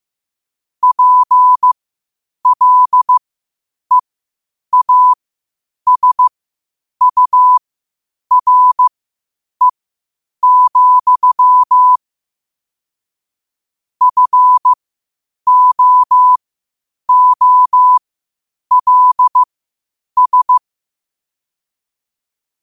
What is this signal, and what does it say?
Morse code, heard as a keyed tone, about 15 wpm, spaced as for 8 wpm, 1000 Hz. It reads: PLEASURE, FOOLS
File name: qotdfiles / 8wpm